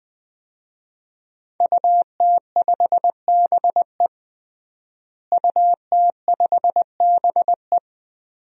Morse code, heard as a keyed tone, about 20 wpm, 700 Hz. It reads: UT5BE UT5BE
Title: DXCC-100 Call Signs - 5 Characters - Repeated 2x - 20wpm